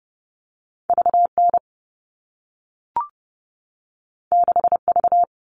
Morse code, heard as a keyed tone, about 30 wpm, 700 Hz. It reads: VN 6V
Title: All Cut Numbers - 2-Digit 30wpm